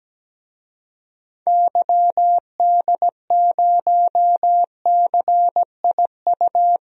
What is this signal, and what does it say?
Morse code, heard as a keyed tone, about 17 wpm, 700 Hz. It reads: YD0CIU
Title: DXCC-100 Call Signs - 6 Characters - 17wpm